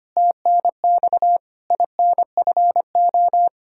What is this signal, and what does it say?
Morse code, heard as a keyed tone, about 25 wpm, 700 Hz. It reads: TNX INFO
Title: Practice QSO - Exchange 3 - More Info - 25wpm